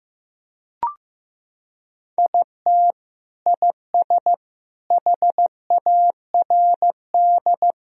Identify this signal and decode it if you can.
Morse code, heard as a keyed tone, about 15 wpm, 700 Hz. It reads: IT IS HARD
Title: Sentences from Top 1000 Words - Review Part 4 - 15wpm